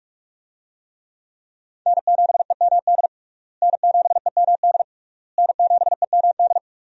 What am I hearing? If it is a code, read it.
Morse code, heard as a keyed tone, about 47 wpm, 700 Hz. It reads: N7EMD N7EMD N7EMD
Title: USA Call Signs - 5 Characters - Repeated 3x - 45wpm